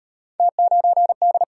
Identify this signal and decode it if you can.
Morse code, heard as a keyed tone, about 35 wpm, 700 Hz. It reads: T9D